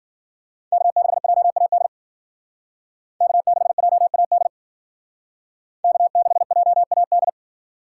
Morse code, heard as a keyed tone, about 52 wpm, 700 Hz. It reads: K6JAD K6JAD K6JAD